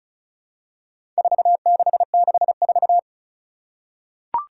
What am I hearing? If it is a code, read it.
Morse code, heard as a keyed tone, about 35 wpm, 700 Hz. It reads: 4664
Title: Four Challenging Letter-Number Combinations 35wpm